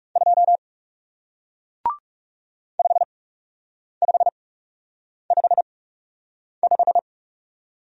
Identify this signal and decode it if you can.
Morse code, heard as a keyed tone, about 39 wpm, spaced as for 8 wpm, 700 Hz. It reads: 2 5555